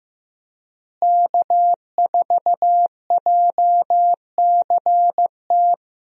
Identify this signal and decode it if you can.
Morse code, heard as a keyed tone, about 16 wpm, 700 Hz. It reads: K4JCT
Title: USA Call Signs - 5 Characters - Repeated 3x - 15wpm